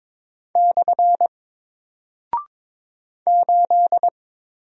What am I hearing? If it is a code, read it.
Morse code, heard as a keyed tone, about 22 wpm, 700 Hz. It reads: / 8
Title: Single Character (Letter:Number:Punctuation:Prosign) 22wpm